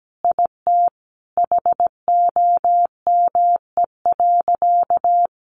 Morse code, heard as a keyed tone, about 17 wpm, 700 Hz. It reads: IT HOME.
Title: Sentences from Top 1500 Words 17wpm